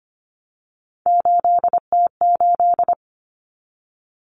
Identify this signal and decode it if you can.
Morse code, heard as a keyed tone, about 25 wpm, 700 Hz. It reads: 8T8
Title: Common Cut Numbers - 3-Digit 25wpm